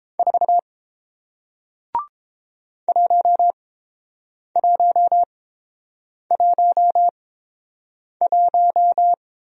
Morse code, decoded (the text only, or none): 4 1111